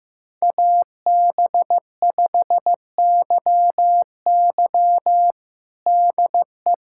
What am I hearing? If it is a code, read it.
Morse code, heard as a keyed tone, about 15 wpm, 700 Hz. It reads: AB5YY DE
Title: USA Callsign DE Callsign - 5 Characters - 15wpm